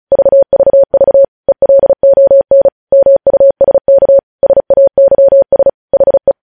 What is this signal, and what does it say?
Morse code, heard as a keyed tone, about 36 wpm, 550 Hz. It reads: VVV ELON MUSK SAYS HE
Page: news headlines in morse code at 35 wpm